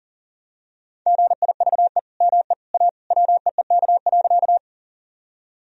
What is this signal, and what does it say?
Morse code, heard as a keyed tone, about 37 wpm, 700 Hz. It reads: GIVE ME A WEEK.
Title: Sentences from Top 1000 Words - Review Part 2 - 40wpm